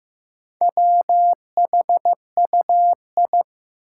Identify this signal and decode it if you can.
Morse code, heard as a keyed tone, about 15 wpm, 700 Hz. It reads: WHUI
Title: Four Challenging Letter-Number Combinations 15wpm